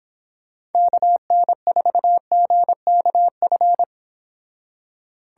Morse code, heard as a keyed tone, about 29 wpm, 700 Hz. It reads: KN4GKF